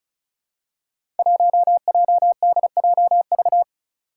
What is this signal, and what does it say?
Morse code, heard as a keyed tone, about 34 wpm, 700 Hz. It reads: 1JDJV
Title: Five Challenging Letter-Number Combinations 35wpm